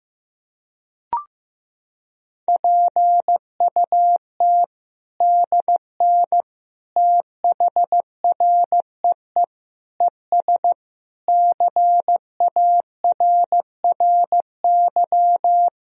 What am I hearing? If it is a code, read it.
Morse code, heard as a keyed tone, about 15 wpm, 700 Hz. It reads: PUT DN THREE ES CARRY
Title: Sentences from Top 400 Words and Abbreviations 15wpm